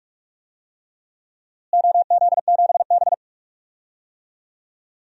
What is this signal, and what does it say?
Morse code, heard as a keyed tone, about 46 wpm, 700 Hz. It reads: OZ7B
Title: DXCC-100 Call Signs - 4 Characters - Repeated 3x - 45wpm